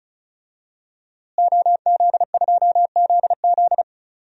Morse code, heard as a keyed tone, about 34 wpm, 700 Hz. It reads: OZ2ZZ
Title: DXCC-100 Call Signs - 5 Characters - Repeated 2x - 35wpm